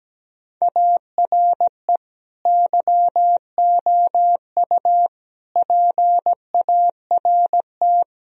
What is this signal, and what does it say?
Morse code, heard as a keyed tone, about 17 wpm, 700 Hz. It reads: ARE YOU PART
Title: Sentences from Top 1000 Words - Review Part 2 - 17wpm